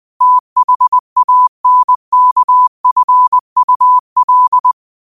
Morse code, heard as a keyed tone, about 20 wpm, 1000 Hz. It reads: THANKFUL
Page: Quotes for Fri, 15 Aug 2025 in Morse Code at 20 words per minute.